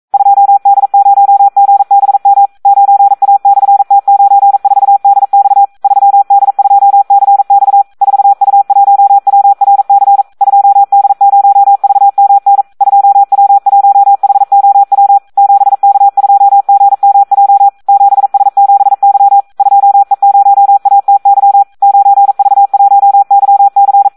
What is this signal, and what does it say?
Morse code, heard as a keyed tone, about 42 wpm, 800 Hz, with radio noise. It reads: 2D0GBM 9A/T94DX 3D2CX 4U1WRC 2D0VMN 2W1HOW 7K2GMJ 7S7Y 2E0ATY 9V1YC
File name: Audio practica CW / SD 1 42